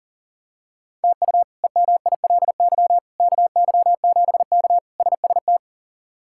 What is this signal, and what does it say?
Morse code, heard as a keyed tone, about 40 wpm, 700 Hz. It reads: TU EMILY KY7K SST
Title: K1USN SST QSO Practice - 40wpm